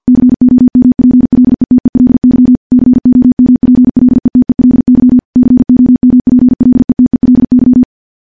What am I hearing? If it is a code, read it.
Morse code, heard as a keyed tone, about 49 wpm, 250 Hz. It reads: COMPLETELY COMPLETELY COMPLETELY
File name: morse / morse-training-50WPM